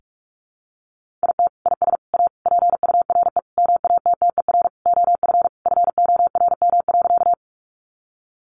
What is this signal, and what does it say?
Morse code, heard as a keyed tone, about 44 wpm, 700 Hz. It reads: IT IS A PURE MATTER OF FORM.
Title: Sentences from Top 1500 Words 45wpm